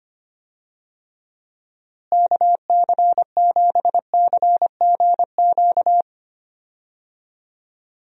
Morse code, heard as a keyed tone, about 25 wpm, 700 Hz. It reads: KC7CGQ
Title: USA Call Signs - 6 Characters - Repeated 2x - 25wpm